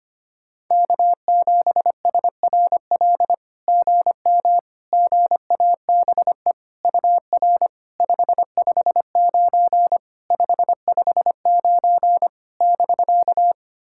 Morse code, heard as a keyed tone, about 25 wpm, 690 Hz. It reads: K7SRL GM GABE UR 559 559 <BK>
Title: SOTA QSO Practice 25wpm